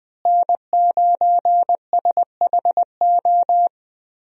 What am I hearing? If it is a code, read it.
Morse code, heard as a keyed tone, about 20 wpm, 700 Hz. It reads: N9SHO
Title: USA Call Signs - 5 Characters - Repeated 2x - 20wpm